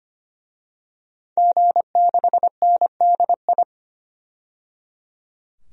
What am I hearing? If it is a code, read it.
Morse code, heard as a keyed tone, about 25 wpm, 700 Hz. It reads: G6NDI
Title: DXCC-100 Call Signs - 5 Characters - 25wpm